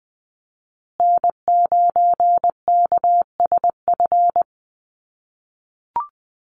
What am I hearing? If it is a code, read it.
Morse code, heard as a keyed tone, about 20 wpm, 700 Hz. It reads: N9KSF
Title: USA Call Signs Repeated 2x 20wpm